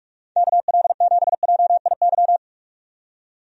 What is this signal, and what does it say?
Morse code, heard as a keyed tone, about 43 wpm, 700 Hz. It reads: KL7JIY
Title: USA Call Signs - 6 Characters - 45wpm